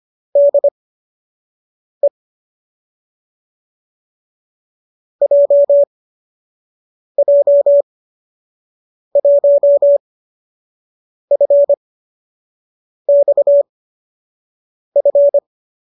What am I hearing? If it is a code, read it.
Morse code, heard as a keyed tone, about 25 wpm, spaced as for 6 wpm, 570 Hz. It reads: DE JJ1FXF